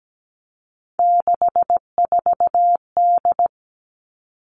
Morse code, heard as a keyed tone, about 17 wpm, 700 Hz. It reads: 64D
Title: Three Challenging Letter-Number Combinations 17wpm